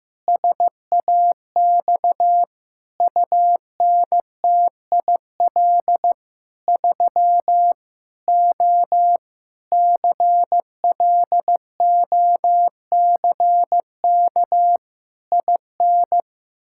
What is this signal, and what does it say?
Morse code, heard as a keyed tone, about 15 wpm, 700 Hz. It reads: SAX UNTIL 3 O CLOCK IN
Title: Pangram Challenge v2 - 15wpm